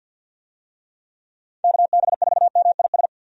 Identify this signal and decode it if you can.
Morse code, heard as a keyed tone, about 52 wpm, 700 Hz. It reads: KB4MIS
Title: USA Call Signs - 6 Characters - 50wpm